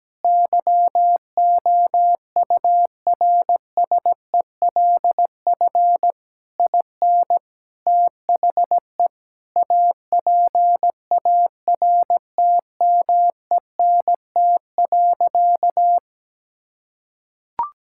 Morse code, heard as a keyed tone, about 17 wpm, 700 Hz. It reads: YOURSELF IN THE APARTMENT.
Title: Sentences from Top 1000 Words - Review Part 6 - 17wpm